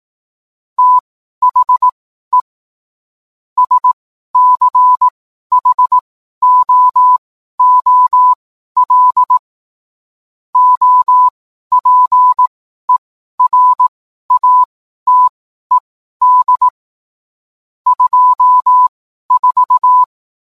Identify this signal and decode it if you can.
Morse code, heard as a keyed tone, about 18 wpm, spaced as for 12 wpm, 1000 Hz. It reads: THE SCHOOL OPERATED 24
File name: cwblog / cw mp3 directory / mem morse 12